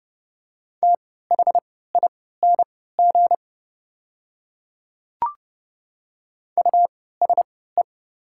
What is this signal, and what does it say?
Morse code, heard as a keyed tone, about 30 wpm, spaced as for 17 wpm, 700 Hz. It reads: THING USE
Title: 3x Character Spacing - Top 100 Words - 30wpm